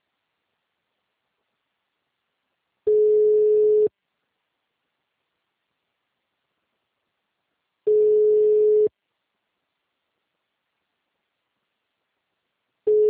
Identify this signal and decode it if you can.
Ringback tone (Europe, 425 Hz).